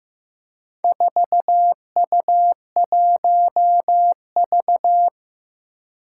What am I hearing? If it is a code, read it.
Morse code, heard as a keyed tone, about 15 wpm, 700 Hz. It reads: 4U1V
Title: Four Challenging Letter-Number Combinations 15wpm